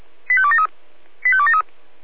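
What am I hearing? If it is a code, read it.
Five-tone selective call (ZVEI), heard as quick runs of five tones, tones of about 70 ms. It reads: 85273 85273
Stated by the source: ZVEI five-tone call 85215 received by SDR.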